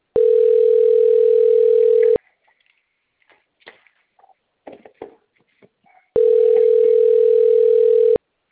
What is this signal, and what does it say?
Ringback tone (North America), 440 + 480 Hz.